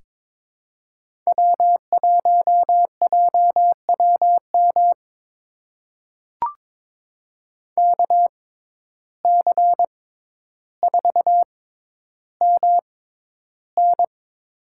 Morse code, heard as a keyed tone, about 22 wpm, spaced as for 11 wpm, 700 Hz. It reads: W1JWM KC4MN
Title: USA Call Signs - 5 Characters - 6x Character Spacing - 22wpm